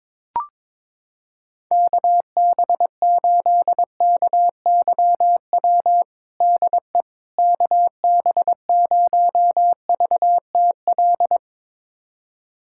Morse code, heard as a keyed tone, about 22 wpm, 700 Hz. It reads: KB8KYW DE KB0VTL
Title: USA Callsign DE Callsign 22wpm